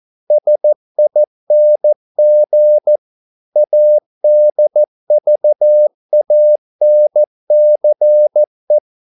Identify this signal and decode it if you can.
Morse code, heard as a keyed tone, about 14 wpm, 600 Hz. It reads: SING ADVANCE